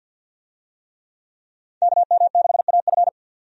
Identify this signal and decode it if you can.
Morse code, heard as a keyed tone, about 52 wpm, 700 Hz. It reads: KM6AF
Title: USA Call Signs - 5 Characters - 50wpm